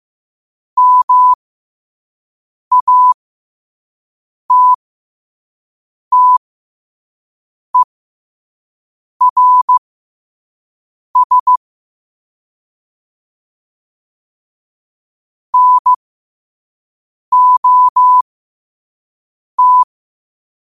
Morse code, heard as a keyed tone, about 15 wpm, spaced as for 5 wpm, 1000 Hz. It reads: MATTERS NOT
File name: qotdfiles / 5wpm